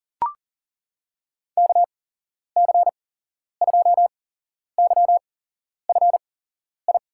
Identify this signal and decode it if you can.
Morse code, heard as a keyed tone, about 38 wpm, spaced as for 11 wpm, 700 Hz. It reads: KC2YFI
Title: USA Call Signs - 6 Characters - 8x Character Spacing - 40wpm